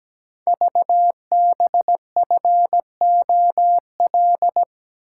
Morse code, heard as a keyed tone, about 17 wpm, 700 Hz. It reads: VBFOL
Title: Code Groups - 5 Letters - 17wpm